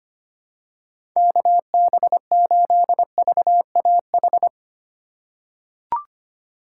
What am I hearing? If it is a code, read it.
Morse code, heard as a keyed tone, about 25 wpm, 700 Hz. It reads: KB8VAH
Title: USA Call Signs - 2x Character Spacing - 25wpm